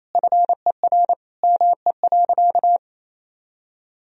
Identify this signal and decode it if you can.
Morse code, heard as a keyed tone, about 28 wpm, 700 Hz. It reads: FER ME.
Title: Sentences from Top 300 Words and Abbreviations 28wpm